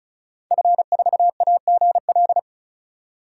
Morse code, heard as a keyed tone, about 34 wpm, 700 Hz. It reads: F4AGL
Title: DXCC-100 Call Signs - 5 Characters - 35wpm